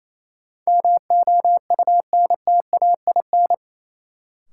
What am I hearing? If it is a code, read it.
Morse code, heard as a keyed tone, about 28 wpm, 700 Hz. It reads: MOUNTAIN
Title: Sentences from Top 1000 Words - Review Part 1 - New Word Drill - 28wpm